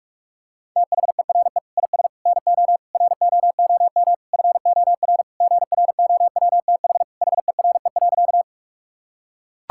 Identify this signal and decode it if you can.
Morse code, heard as a keyed tone, about 41 wpm, 690 Hz. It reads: THERE IS NO ROOM FOR GROWTH HERE.
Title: Sentences from Top 1000 Words - Review Part 5 - 45wpm